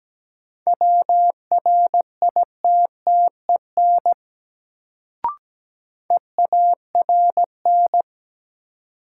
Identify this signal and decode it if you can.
Morse code, heard as a keyed tone, about 17 wpm, 700 Hz. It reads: WRITTEN EARN
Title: Sentences from Top 1500 Words - New Word Drill Alt - 17wpm